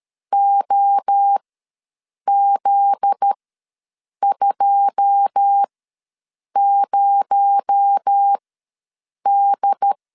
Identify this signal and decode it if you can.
Morse code, heard as a keyed tone, about 13 wpm, spaced as for 7 wpm, 800 Hz. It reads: OZ20D
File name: LEZIONI / 9 2867 ws 12wpm 3431